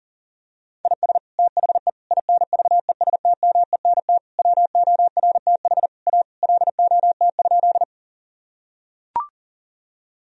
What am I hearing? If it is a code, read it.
Morse code, heard as a keyed tone, about 40 wpm, 700 Hz. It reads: IS THE INVESTMENT WORTH A LOT?
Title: Sentences from Top 1000 Words - Review Part 4 - 40wpm